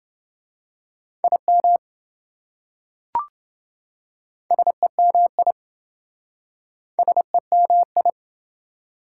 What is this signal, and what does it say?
Morse code, heard as a keyed tone, about 27 wpm, 700 Hz. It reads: IM SEMI SEMI